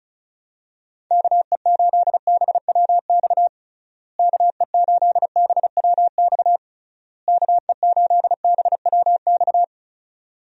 Morse code, heard as a keyed tone, about 36 wpm, 700 Hz. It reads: KE8BWX KE8BWX KE8BWX
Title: USA Call Signs - 6 Characters - Repeated 3x - 35wpm